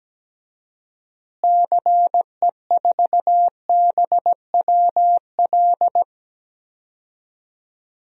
Morse code, heard as a keyed tone, about 17 wpm, 700 Hz. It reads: CE4BWL
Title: DXCC-100 Call Signs - 6 Characters - Repeated 2x - 17wpm